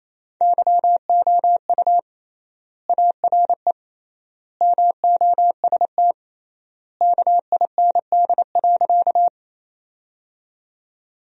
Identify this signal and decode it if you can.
Morse code, heard as a keyed tone, about 28 wpm, 700 Hz. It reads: YOU ARE MOST KIND.
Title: 3x Word-Spacing - Sentences from Top 200 Words - 28wpm